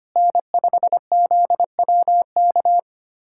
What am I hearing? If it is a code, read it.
Morse code, heard as a keyed tone, about 25 wpm, 700 Hz. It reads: N5ZWK
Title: USA Callsign DE Callsign - 5 Characters - 25wpm